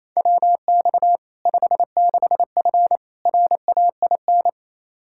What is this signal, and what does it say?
Morse code, heard as a keyed tone, about 28 wpm, 700 Hz. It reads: WX 56F RAIN
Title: Practice QSO - Exchange 2 - Info - 28wpm